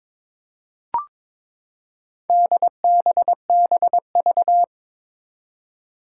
Morse code, heard as a keyed tone, about 22 wpm, 700 Hz. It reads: DBBV
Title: Four Challenging Letter-Number Combinations 22wpm